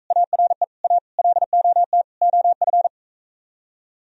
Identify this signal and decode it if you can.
Morse code, heard as a keyed tone, about 45 wpm, 700 Hz. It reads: ARE A LOT OF